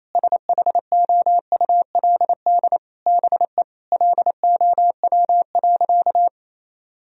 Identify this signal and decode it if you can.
Morse code, heard as a keyed tone, about 28 wpm, 700 Hz. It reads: SHOULD BE LOW.
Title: Sentences from Top 1000 Words - Review Part 1 - 28wpm